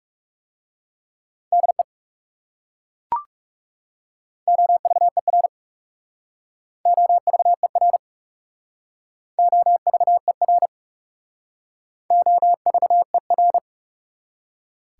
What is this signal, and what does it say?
Morse code, heard as a keyed tone, about 38 wpm, 690 Hz. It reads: DE OVER OVER OVER OVER